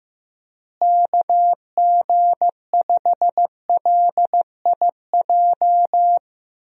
Morse code, heard as a keyed tone, about 15 wpm, 700 Hz. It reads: KG5LIJ
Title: USA Call Signs - 6 Characters - Repeated 3x - 15wpm